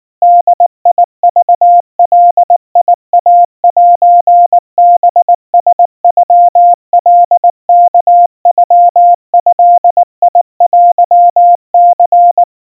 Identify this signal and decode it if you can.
Morse code (Japanese Wabun code), heard as a keyed tone, about 19 wpm, 700 Hz. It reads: ボクガイセハラノカワノドテニ
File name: kawa / Morse sample / kamyu 70 cpm 700 Hz Crispy